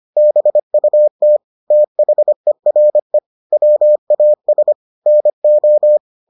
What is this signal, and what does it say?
Morse code, heard as a keyed tone, about 25 wpm, 600 Hz. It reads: BUT THERE WAS NO